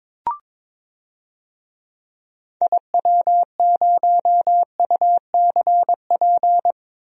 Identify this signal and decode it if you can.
Morse code, heard as a keyed tone, about 22 wpm, 700 Hz. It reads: IW0UCP
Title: DXCC-100 Call Signs - 6 Characters - Repeated 3x - 22wpm